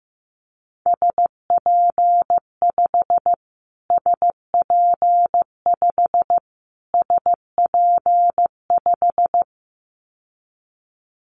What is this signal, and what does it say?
Morse code, heard as a keyed tone, about 15 wpm, 700 Hz. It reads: SP5 SP5 SP5
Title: Callsign Prefixes for Top 25 DXCC - Repeated 3x - 15wpm